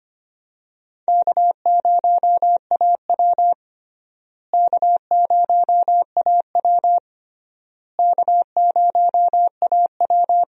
Morse code, heard as a keyed tone, about 25 wpm, 700 Hz. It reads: K0AW K0AW K0AW
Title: USA Call Signs - 4 Characters - Repeated 3x - 25wpm